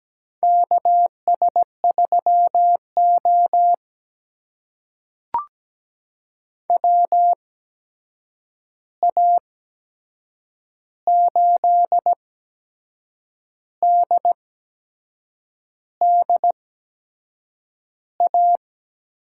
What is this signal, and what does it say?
Morse code, heard as a keyed tone, about 17 wpm, spaced as for 6 wpm, 700 Hz. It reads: KS3O WA8DDA
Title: USA Call Signs - 8x Character Spacing - 17wpm